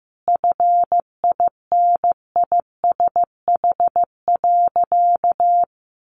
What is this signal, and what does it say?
Morse code, heard as a keyed tone, about 15 wpm, 700 Hz. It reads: FINISH.
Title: Sentences from Top 1200 Words 15wpm